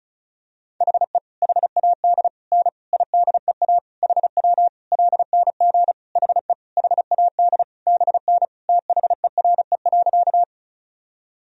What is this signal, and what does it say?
Morse code, heard as a keyed tone, about 35 wpm, 700 Hz. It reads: HE HAD N IDEA HW LNG HE HAD BN THERE.
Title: Sentences from Top 300 Words and Abbreviations 35wpm